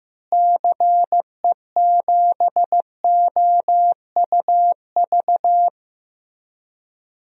Morse code, heard as a keyed tone, about 15 wpm, 700 Hz. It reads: CE7OUV
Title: DXCC-100 Call Signs - 6 Characters - Repeated 3x - 15wpm